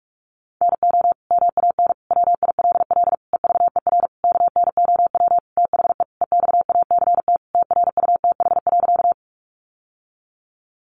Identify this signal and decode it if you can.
Morse code, heard as a keyed tone, about 44 wpm, 700 Hz. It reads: NO MAN WILL EVER KNOW THE EXACT TRUTH.
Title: Sentences from Top 1300 Words 45wpm